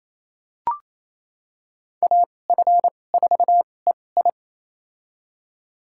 Morse code, heard as a keyed tone, about 28 wpm, spaced as for 20 wpm, 700 Hz. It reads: AF4EI
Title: USA Call Signs - 5 Characters - 2x Character Spacing - 28wpm